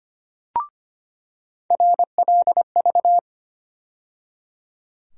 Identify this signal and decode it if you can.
Morse code, heard as a keyed tone, about 25 wpm, 700 Hz. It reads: RLV
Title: USA Call Sign Postfixes - 3 Letters 25wpm